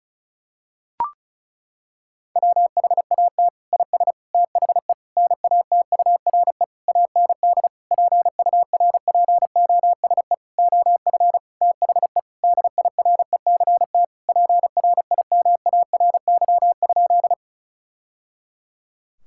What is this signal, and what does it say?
Morse code, heard as a keyed tone, about 36 wpm, 700 Hz. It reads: WHAT IS THE NATURE AND PURPOSE OF THE DIRECT PRIMARY?
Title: Sentences from Top 1000 Words - Review Part 5 - 35wpm